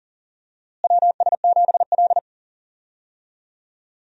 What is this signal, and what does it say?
Morse code, heard as a keyed tone, about 40 wpm, 700 Hz. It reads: WS7L
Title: USA Call Signs - 4 Characters - Repeated 2x - 40wpm